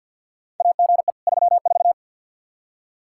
Morse code, heard as a keyed tone, about 51 wpm, 700 Hz. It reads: AGE 34
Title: Practice QSO - Exchange 3 - More Info - 50wpm